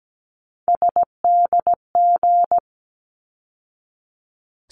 Morse code, heard as a keyed tone, about 17 wpm, 700 Hz. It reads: SDG
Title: ARRL Sections 17wpm